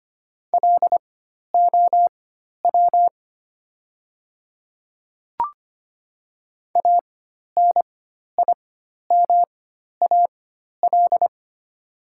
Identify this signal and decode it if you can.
Morse code, heard as a keyed tone, about 25 wpm, spaced as for 12 wpm, 700 Hz. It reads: LOW ANIMAL
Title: 4x Character Spacing - Top 200 Words - 25wpm